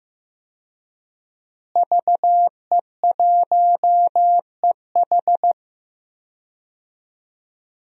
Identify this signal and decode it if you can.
Morse code, heard as a keyed tone, about 15 wpm, 700 Hz. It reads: VE1EH
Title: DXCC-100 Call Signs - 5 Characters - Repeated 2x - 15wpm